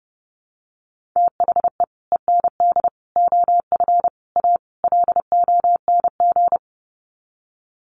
Morse code, heard as a keyed tone, about 30 wpm, 700 Hz. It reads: THE END OF A LONG
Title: Sets of 5 Words Encore2 30wpm